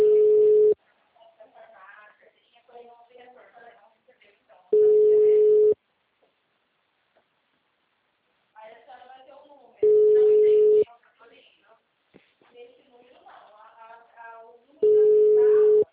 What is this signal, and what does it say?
Ringback tone (Europe, 425 Hz), with speech.